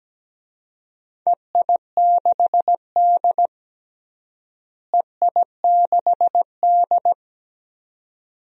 Morse code, heard as a keyed tone, about 17 wpm, 700 Hz. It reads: EI6D EI6D
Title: DXCC-100 Call Signs - 4 Characters - Repeated 2x - 17wpm